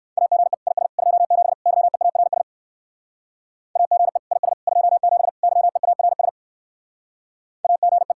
Morse code, heard as a keyed tone, about 53 wpm, 700 Hz. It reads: AGE IS 27 YEARS AGE IS 27 YEARS AGE
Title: Speed-Racing Common QSO Phrases - 45wpm